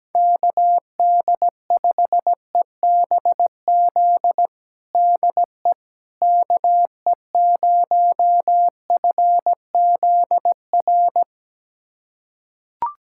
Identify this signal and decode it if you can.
Morse code, heard as a keyed tone, about 17 wpm, 700 Hz. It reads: KD5EBZ DE KE0FZR
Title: USA Callsign DE Callsign - 6 Characters - 17wpm